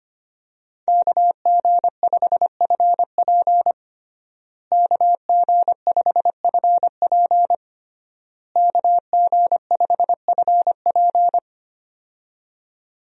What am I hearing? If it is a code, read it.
Morse code, heard as a keyed tone, about 25 wpm, 700 Hz. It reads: KG5FP KG5FP KG5FP
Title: USA Call Signs - 5 Characters - Repeated 3x - 25wpm